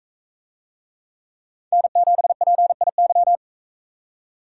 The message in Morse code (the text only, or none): N7PIY